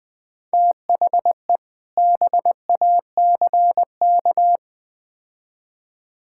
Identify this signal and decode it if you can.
Morse code, heard as a keyed tone, about 20 wpm, 700 Hz. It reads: THE BACK
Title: Build A Phrase - 4-Word Phrases - 20wpm